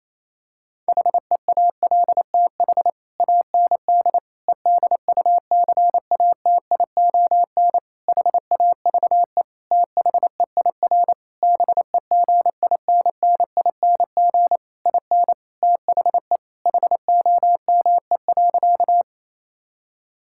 Morse code, heard as a keyed tone, about 27 wpm, 700 Hz. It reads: HEALTH AND EDUCATION HAVE THEIR BEGINNING IN THE HOME.
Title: Sentences from Top 1000 Words - Review Part 6 - 28wpm